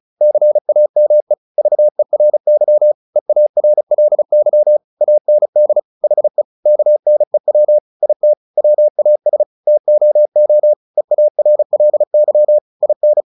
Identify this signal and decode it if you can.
Morse code, heard as a keyed tone, about 34 wpm, 600 Hz. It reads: CAME VERY EARLY AND HE KNEW IT WAS TOO EARLY IN
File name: the old man and the sea / the old man and the sea 1 35wpm